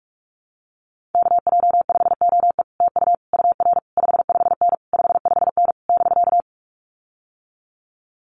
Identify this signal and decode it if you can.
Morse code, heard as a keyed tone, about 44 wpm, 700 Hz. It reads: KJ5OE TU UR 55N 55N <BK>
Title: Common QSO Phrases Encore 45wpm